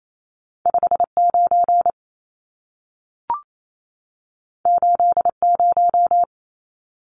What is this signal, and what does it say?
Morse code, heard as a keyed tone, about 28 wpm, 700 Hz. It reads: 59 80
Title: Numbers - 2-Digit 28wpm